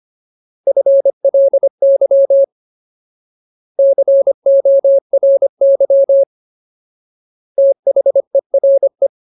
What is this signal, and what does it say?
Morse code, heard as a keyed tone, about 25 wpm, 550 Hz. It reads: FLY CORY THERE
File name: Ultimate Word List / 25WPM Ultimate